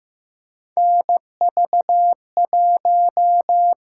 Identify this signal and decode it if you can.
Morse code, heard as a keyed tone, about 15 wpm, 700 Hz. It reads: NV1
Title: USA Call Signs - 6x Character Spacing - 15wpm